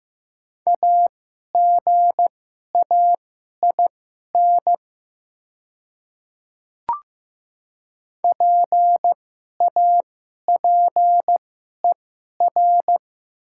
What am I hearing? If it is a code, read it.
Morse code, heard as a keyed tone, about 15 wpm, spaced as for 11 wpm, 700 Hz. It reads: AGAIN PAPER
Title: 2x Character Spacing - Top 500 Words - 15wpm